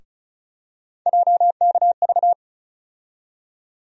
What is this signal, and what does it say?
Morse code, heard as a keyed tone, about 36 wpm, 700 Hz. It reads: JKV
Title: USA Call Sign Postfixes - 3 Letters 35wpm